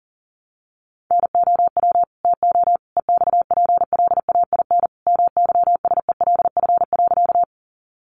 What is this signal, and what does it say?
Morse code, heard as a keyed tone, about 40 wpm, 700 Hz. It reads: NOW TO EXPLAIN MYSELF.
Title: Sentences from Top 1300 Words 40wpm